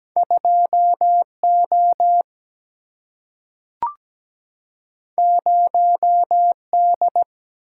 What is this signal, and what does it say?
Morse code, heard as a keyed tone, about 17 wpm, 700 Hz. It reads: 2O 0D
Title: Single-Letter Single-Number 17wpm